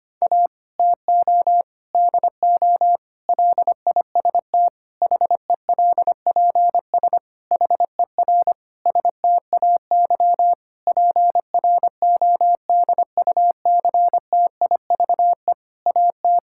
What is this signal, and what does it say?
Morse code, heard as a keyed tone, about 25 wpm, 700 Hz. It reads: A TO DO LIST HELPS HER STAY PRODUCTIVE AT
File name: practice / 25wpm / Difficult Sentences with Hints - 25wpm